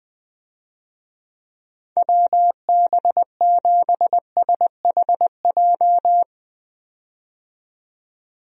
Morse code, heard as a keyed tone, about 21 wpm, 700 Hz. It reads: WB7SHJ
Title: USA Call Signs - 6 Characters - 20wpm